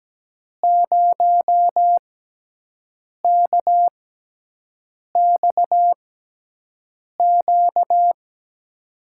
Morse code, heard as a keyed tone, about 17 wpm, spaced as for 6 wpm, 700 Hz. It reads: 0KXQ
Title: USA Call Signs - 6 Characters - 6x Character Spacing - 17wpm